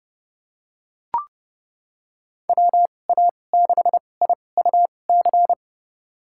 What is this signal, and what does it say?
Morse code, heard as a keyed tone, about 30 wpm, spaced as for 22 wpm, 700 Hz. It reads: WA6IUC
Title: USA Call Signs - 6 Characters - 2x Character Spacing - 30wpm